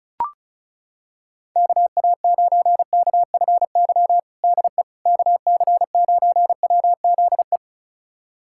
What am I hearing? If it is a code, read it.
Morse code, heard as a keyed tone, about 34 wpm, 700 Hz. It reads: KA9KFY DE KC9WZE
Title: USA Callsign DE Callsign - 6 Characters - 35wpm